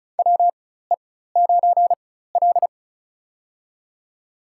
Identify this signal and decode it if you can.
Morse code, heard as a keyed tone, about 34 wpm, spaced as for 16 wpm, 700 Hz. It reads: WE9L
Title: USA Call Signs - 4x Character Spacing - 35wpm